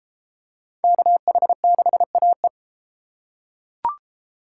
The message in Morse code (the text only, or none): KH6AE